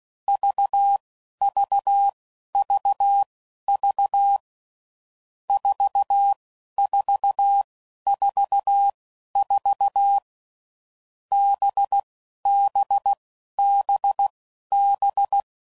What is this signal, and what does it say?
Morse code, heard as a keyed tone, about 16 wpm, spaced as for 11 wpm, 800 Hz. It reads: VVVV 4444 BBBB
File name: Groups / Basic Letter Groups / VB46